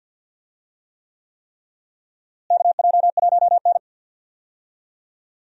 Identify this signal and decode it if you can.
Morse code, heard as a keyed tone, about 46 wpm, 700 Hz. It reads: KJ1N
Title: USA Call Signs - 4 Characters - Repeated 3x - 50wpm